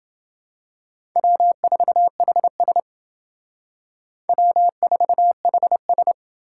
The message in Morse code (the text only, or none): W4HS W4HS